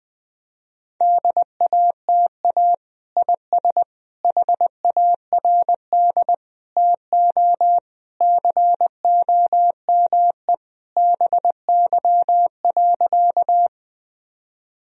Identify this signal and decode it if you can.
Morse code, heard as a keyed tone, about 20 wpm, 700 Hz. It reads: DATA IS HARD TO COME BY.
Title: Sentences from Top 1000 Words - Review Part 3 - 20wpm